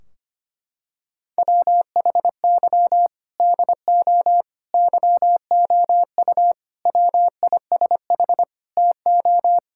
Morse code, heard as a keyed tone, about 25 wpm, 700 Hz. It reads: WHY DO YOU WISH TO
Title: Sentences from Top 1600 Words 25wpm